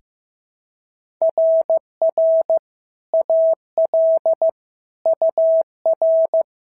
Morse code, heard as a keyed tone, about 14 wpm, 650 Hz. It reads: RR AL UR
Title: SOTA QSO Practice 15wpm